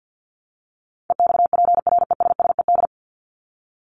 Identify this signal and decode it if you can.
Morse code, heard as a keyed tone, about 50 wpm, 700 Hz. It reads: EXPRESSED
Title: ED Suffix - Top 100 Words with Base Words 50wpm